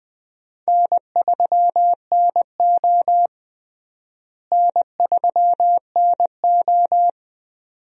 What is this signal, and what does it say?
Morse code, heard as a keyed tone, about 20 wpm, 700 Hz. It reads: N3NO N3NO
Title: USA Call Signs - 4 Characters - Repeated 2x - 20wpm